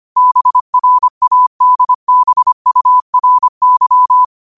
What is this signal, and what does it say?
Morse code, heard as a keyed tone, about 25 wpm, 1000 Hz. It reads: BRADBURY
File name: qotdfiles / 25wpm